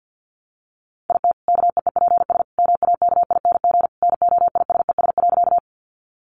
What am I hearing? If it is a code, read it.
Morse code, heard as a keyed tone, about 50 wpm, 700 Hz. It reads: IT KEEPS MAKING NOISES.
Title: Additional Easy Phrases 50wpm